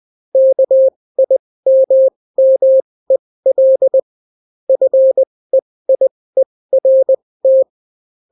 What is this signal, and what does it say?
Morse code, heard as a keyed tone, about 20 wpm, spaced as for 16 wpm, 530 Hz. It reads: KIMMEL FEIERT